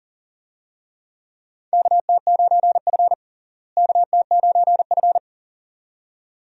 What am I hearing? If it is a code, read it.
Morse code, heard as a keyed tone, about 40 wpm, 700 Hz. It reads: KT9F KT9F
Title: USA Call Signs - 4 Characters - Repeated 2x - 40wpm